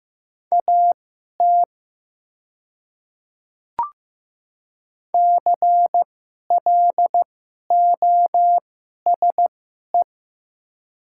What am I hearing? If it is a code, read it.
Morse code, heard as a keyed tone, about 15 wpm, spaced as for 11 wpm, 700 Hz. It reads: AT CLOSE
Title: 2x Character Spacing - Top 300 Words - 15wpm